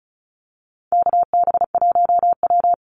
Morse code, heard as a keyed tone, about 34 wpm, 700 Hz. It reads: KB1W
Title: Call Signs Repeated 3x - 35wpm